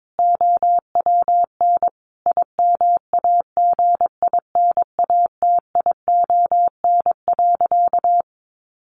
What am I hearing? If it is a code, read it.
Morse code, heard as a keyed tone, about 22 wpm, 700 Hz. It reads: OWN IMAGINATION.
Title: Sentences from Top 1300 Words 22wpm